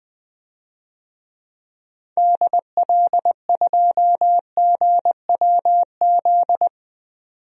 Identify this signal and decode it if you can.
Morse code, heard as a keyed tone, about 20 wpm, 700 Hz. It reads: DL2GWZ
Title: DXCC-100 Call Signs - 6 Characters - 20wpm